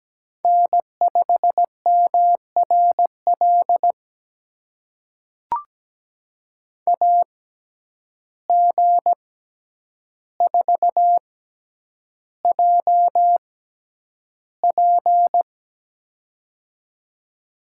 Morse code, heard as a keyed tone, about 17 wpm, spaced as for 9 wpm, 700 Hz. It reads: N5MRL AG4JP
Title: USA Call Signs - 5 Characters - 6x Character Spacing - 17wpm